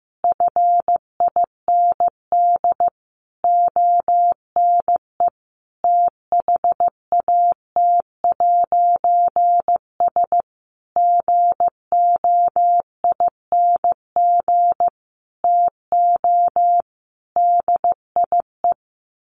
Morse code, heard as a keyed tone, about 15 wpm, 700 Hz. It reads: FIND ONE THAT'S GOING TO DIE
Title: A Sound Of Thunder 15wpm